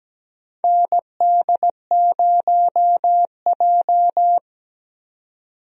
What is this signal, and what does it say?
Morse code, heard as a keyed tone, about 17 wpm, 700 Hz. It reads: ND0J
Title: USA Call Signs - 4 Characters - 17wpm